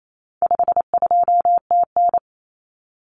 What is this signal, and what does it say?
Morse code, heard as a keyed tone, about 28 wpm, 700 Hz. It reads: 52TN
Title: Common Cut Numbers - 4-Digit 28wpm